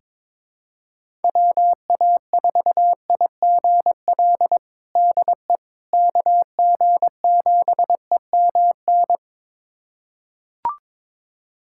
Morse code, heard as a keyed tone, about 21 wpm, 700 Hz. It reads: WA4IGL DE KG7EMN
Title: USA Callsign DE Callsign - 6 Characters - 22wpm